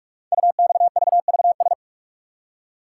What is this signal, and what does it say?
Morse code, heard as a keyed tone, about 44 wpm, 700 Hz. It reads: UXVVS
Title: Code Groups - 5 Letters - 45wpm